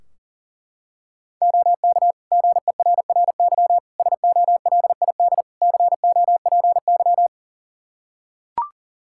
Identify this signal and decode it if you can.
Morse code, heard as a keyed tone, about 40 wpm, 700 Hz. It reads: OK GERRY SOLID COPY
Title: Practice QSO - Exchange 3 - More Info - 40wpm